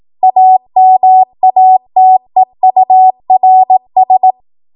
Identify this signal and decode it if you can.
Morse code, heard as a keyed tone, about 18 wpm, 750 Hz. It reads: AMATEURS
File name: Morse / 18WPM